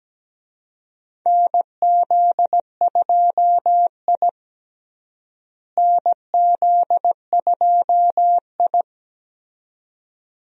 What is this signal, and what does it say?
Morse code, heard as a keyed tone, about 17 wpm, 700 Hz. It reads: NZ2I NZ2I
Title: USA Call Signs - 4 Characters - Repeated 2x - 17wpm